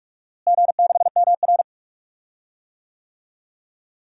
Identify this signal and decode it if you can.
Morse code, heard as a keyed tone, about 47 wpm, 700 Hz. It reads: G6MR
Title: DXCC-100 Call Signs - 4 Characters - Repeated 2x - 45wpm